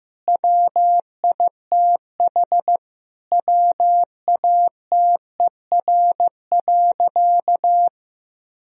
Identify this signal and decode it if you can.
Morse code, heard as a keyed tone, about 15 wpm, 700 Hz. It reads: WITH WATER.
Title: Sentences from Top 1000 Words - Review Part 3 - 15wpm